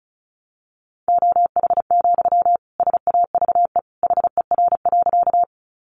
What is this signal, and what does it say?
Morse code, heard as a keyed tone, about 34 wpm, 700 Hz. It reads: OH, SAVE HER.
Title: Sentences from Top 1100 Words 35wpm